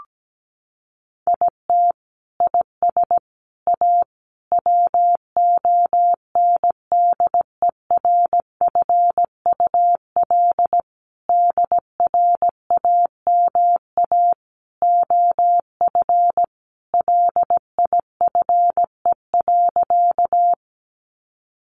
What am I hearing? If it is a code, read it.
Morse code, heard as a keyed tone, about 17 wpm, 700 Hz. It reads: IT IS A WONDERFUL DRAMA OF LIFE.
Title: Sentences from Top 1500 Words 17wpm